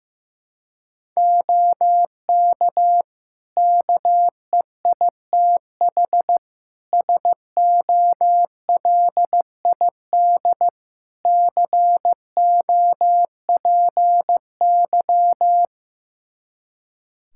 Morse code, heard as a keyed tone, about 15 wpm, 700 Hz. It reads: OK KEITH SOLID COPY
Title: Practice QSO - Exchange 3 - More Info - 15wpm